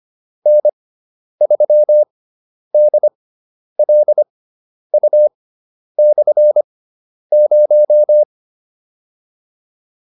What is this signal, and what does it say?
Morse code, heard as a keyed tone, about 25 wpm, spaced as for 10 wpm, 600 Hz. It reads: N3DLU/0